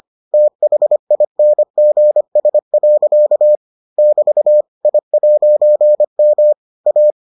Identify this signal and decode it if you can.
Morse code, heard as a keyed tone, about 25 wpm, 600 Hz. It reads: THINGS. <BT> I'M A